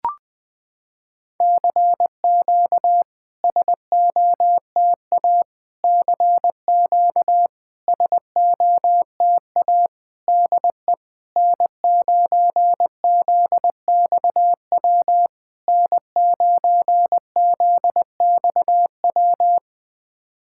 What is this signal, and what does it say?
Morse code, heard as a keyed tone, about 20 wpm, 700 Hz. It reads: CQ SOTA CQ SOTA DE N9ZXW N9ZXW
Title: SOTA QSO Practice 20wpm